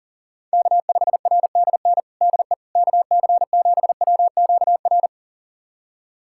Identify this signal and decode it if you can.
Morse code, heard as a keyed tone, about 40 wpm, 700 Hz. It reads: K5RDN DE KC7WQR
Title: USA Callsign DE Callsign 40wpm